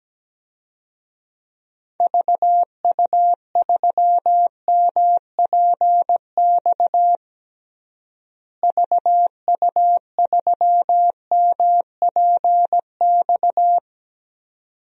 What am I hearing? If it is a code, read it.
Morse code, heard as a keyed tone, about 17 wpm, 700 Hz. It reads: VU3MPX VU3MPX
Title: DXCC-100 Call Signs - 6 Characters - Repeated 2x - 17wpm